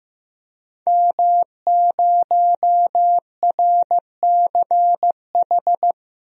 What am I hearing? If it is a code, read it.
Morse code, heard as a keyed tone, about 15 wpm, 700 Hz. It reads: M0RCH
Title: DXCC-100 Call Signs - 5 Characters - 15wpm